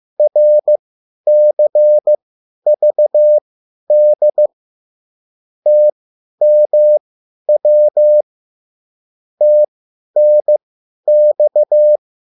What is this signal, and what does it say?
Morse code, heard as a keyed tone, about 15 wpm, spaced as for 10 wpm, 600 Hz. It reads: RCVD TMW TNX
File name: QSL / CW lesson